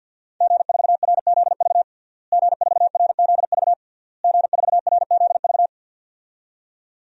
Morse code, heard as a keyed tone, about 52 wpm, 700 Hz. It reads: G4RZV G4RZV G4RZV
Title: DXCC-100 Call Signs - 5 Characters - Repeated 3x - 50wpm